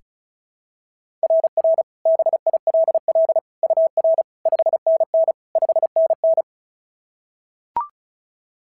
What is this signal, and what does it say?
Morse code, heard as a keyed tone, about 36 wpm, 690 Hz. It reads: RR BILL UR 5NN 5NN
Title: SOTA QSO Practice 35wpm